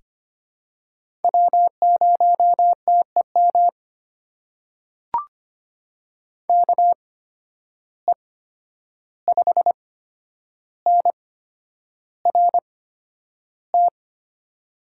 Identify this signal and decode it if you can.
Morse code, heard as a keyed tone, about 25 wpm, spaced as for 10 wpm, 700 Hz. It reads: W0TEM KE5NRT
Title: USA Call Signs - 8x Character Spacing - 25wpm